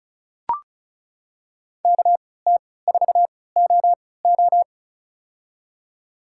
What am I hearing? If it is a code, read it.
Morse code, heard as a keyed tone, about 36 wpm, spaced as for 20 wpm, 700 Hz. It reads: KT4OO
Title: USA Call Signs - 3x Character Spacing - 35wpm